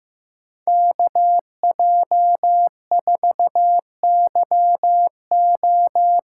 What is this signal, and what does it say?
Morse code, heard as a keyed tone, about 15 wpm, 700 Hz. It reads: KJ4YO
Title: USA Call Signs - 8x Character Spacing - 15wpm